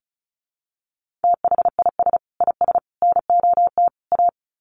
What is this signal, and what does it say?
Morse code, heard as a keyed tone, about 34 wpm, 700 Hz. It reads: THIS IS NOT A
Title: Sets of 4 Words 35wpm